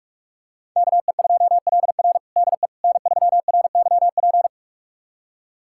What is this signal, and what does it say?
Morse code, heard as a keyed tone, about 44 wpm, 700 Hz. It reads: KE2LR DE N3RYP
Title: USA Callsign DE Callsign - 5 Characters - 45wpm